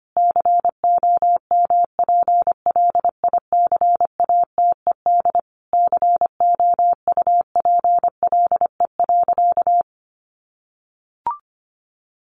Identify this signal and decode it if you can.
Morse code, heard as a keyed tone, about 25 wpm, 700 Hz. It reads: COMPLICATED COUPLE.
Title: Sentences from Top 1200 Words 25wpm